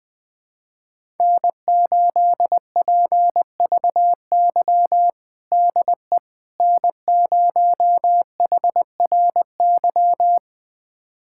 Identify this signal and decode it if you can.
Morse code, heard as a keyed tone, about 20 wpm, 700 Hz. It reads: N8PVY DE N0HRY
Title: USA Callsign DE Callsign - 5 Characters - 20wpm